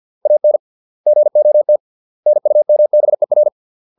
Morse code, heard as a keyed tone, about 50 wpm, 600 Hz. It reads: AN GOT NUMBER
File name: 632 words / 632 Words 50WPM